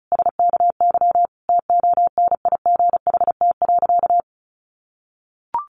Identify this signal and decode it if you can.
Morse code, heard as a keyed tone, about 34 wpm, 700 Hz. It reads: SKY TONIGHT.
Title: Sentences from Top 1300 Words 35wpm